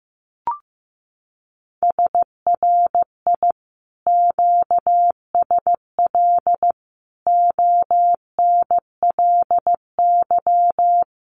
Morse code, heard as a keyed tone, about 15 wpm, 700 Hz. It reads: SRI QSL ONLY
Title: Common QSO Phrases Encore 15wpm